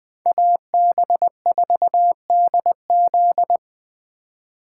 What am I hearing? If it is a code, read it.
Morse code, heard as a keyed tone, about 20 wpm, 700 Hz. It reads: AB4DZ
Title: USA Call Signs - 4x Character Spacing - 20wpm